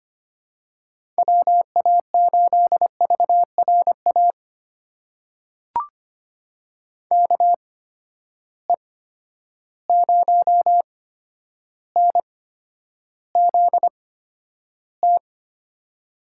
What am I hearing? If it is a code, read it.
Morse code, heard as a keyed tone, about 25 wpm, spaced as for 11 wpm, 700 Hz. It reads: WA8VRA KE0NZT
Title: USA Call Signs - 6 Characters - 8x Character Spacing - 25wpm